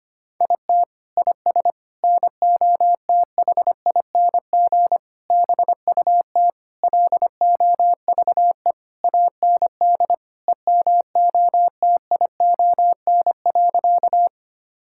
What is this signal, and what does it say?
Morse code, heard as a keyed tone, about 25 wpm, 700 Hz. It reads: IT IS NOTHING BUT LOVE AND EMOTION.
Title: Sentences from Top 1600 Words 25wpm